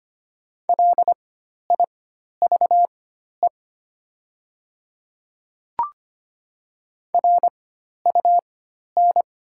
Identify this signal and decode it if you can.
Morse code, heard as a keyed tone, about 25 wpm, spaced as for 12 wpm, 700 Hz. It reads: LIVE RUN
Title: 4x Character Spacing - Top 300 Words - 25wpm